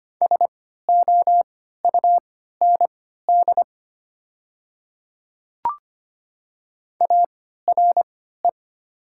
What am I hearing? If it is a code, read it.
Morse code, heard as a keyed tone, about 25 wpm, spaced as for 14 wpm, 700 Hz. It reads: SOUND ARE
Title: 3x Character Spacing - Top 100 Words - 25wpm